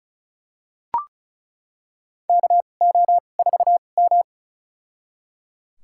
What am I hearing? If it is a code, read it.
Morse code, heard as a keyed tone, about 34 wpm, spaced as for 25 wpm, 700 Hz. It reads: KO4M
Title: USA Call Signs - 4 Characters - 2x Character Spacing - 35wpm